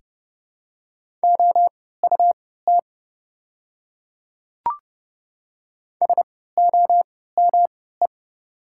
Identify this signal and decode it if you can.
Morse code, heard as a keyed tone, about 30 wpm, spaced as for 17 wpm, 690 Hz. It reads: OUT SOME
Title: 3x Character Spacing - Top 100 Words - 30wpm